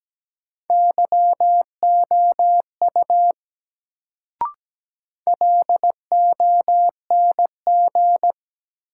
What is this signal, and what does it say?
Morse code, heard as a keyed tone, about 17 wpm, 700 Hz. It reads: YOU LONG
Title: Sets of 2 Words - New Word Drill - 17wpm